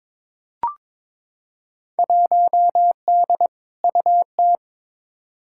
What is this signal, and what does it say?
Morse code, heard as a keyed tone, about 21 wpm, 700 Hz. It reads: 1D UT
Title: Field Day Exchanges - 22wpm